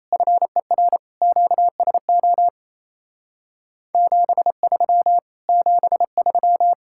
Morse code, heard as a keyed tone, about 34 wpm, 700 Hz. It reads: FER QSO 73 73
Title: Speed-Racing Common QSO Phrases Encore - 28wpm